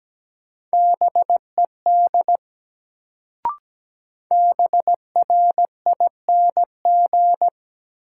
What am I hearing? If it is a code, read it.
Morse code, heard as a keyed tone, about 17 wpm, 700 Hz. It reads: BED BRING
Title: Sets of 5 Words Encore - New Word Drill - 17wpm